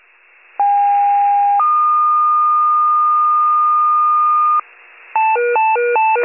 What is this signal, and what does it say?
Two-tone page, heard as a one-second tone, then a three-second tone, about 810 then 1220 Hz. It is heard from 0.6 to 4.6 s.
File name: pager / 73 RESCUE / wcalert